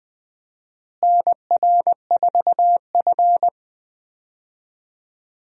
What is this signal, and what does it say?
Morse code, heard as a keyed tone, about 20 wpm, 700 Hz. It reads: NR4F
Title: USA Call Signs - 4 Characters - Repeated 2x - 20wpm